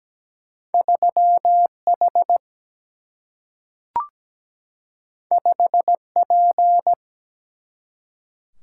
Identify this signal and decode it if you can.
Morse code, heard as a keyed tone, about 17 wpm, 700 Hz. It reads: 3H 5P
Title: Single-Letter Single-Number 17wpm